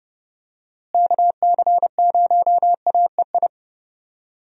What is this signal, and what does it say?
Morse code, heard as a keyed tone, about 30 wpm, 700 Hz. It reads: KC0AEI
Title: USA Call Signs - 6 Characters - Repeated 2x - 30wpm